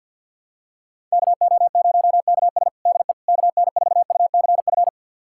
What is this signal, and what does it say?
Morse code, heard as a keyed tone, about 51 wpm, 700 Hz. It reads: KO0KS DE KN4UCF
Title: USA Callsign DE Callsign 50wpm